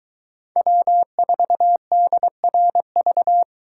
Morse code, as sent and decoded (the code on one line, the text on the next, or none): .-- ....- -.. .-. ...-
W4DRV